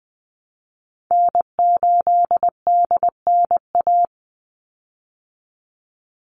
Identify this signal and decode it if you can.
Morse code, heard as a keyed tone, about 20 wpm, 700 Hz. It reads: N8DNA
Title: Call Signs Repeated 3x - 20wpm